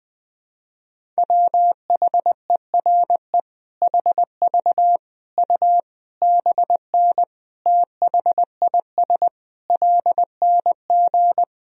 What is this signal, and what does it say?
Morse code, heard as a keyed tone, about 20 wpm, 700 Hz. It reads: WHERE HV U BN THIS LNG
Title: Sentences from Top 300 Words and Abbreviations 20wpm